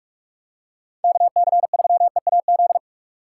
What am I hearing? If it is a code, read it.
Morse code, heard as a keyed tone, about 47 wpm, 700 Hz. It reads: KC3EAZ
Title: USA Call Signs - 6 Characters - 6x Character Spacing - 45wpm